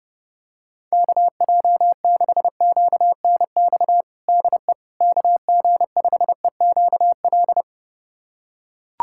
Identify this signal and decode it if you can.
Morse code, heard as a keyed tone, about 32 wpm, 700 Hz. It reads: KJ6QNX DE KG5EQL
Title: USA Callsign DE Callsign - 6 Characters - 30wpm